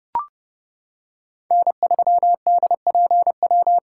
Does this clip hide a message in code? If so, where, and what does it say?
Yes: Morse code from 1.5 to 3.8 s, heard as a keyed tone, about 27 wpm, 700 Hz. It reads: N3DPW